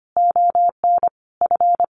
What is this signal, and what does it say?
Morse code, heard as a keyed tone, about 25 wpm, 700 Hz. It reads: ON F
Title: A Sound Of Thunder 25wpm